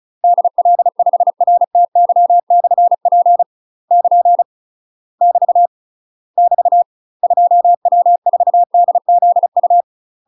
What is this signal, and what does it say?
Morse code, heard as a keyed tone, about 34 wpm, 700 Hz. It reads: DL5RTY/P <KN> <BT> <BT> 2W4DZU
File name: UK MASTER QSOs TRAINING / 35WPM / UK MASTER QSOs 0025